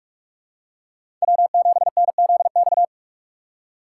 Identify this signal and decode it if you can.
Morse code, heard as a keyed tone, about 42 wpm, 700 Hz. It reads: W7NZX